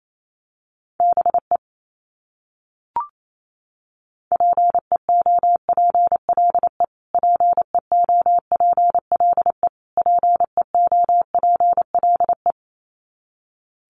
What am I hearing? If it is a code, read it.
Morse code, heard as a keyed tone, about 28 wpm, 700 Hz. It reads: BE PEOPLE PEOPLE PEOPLE
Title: Top 300 Words Repeated 28wpm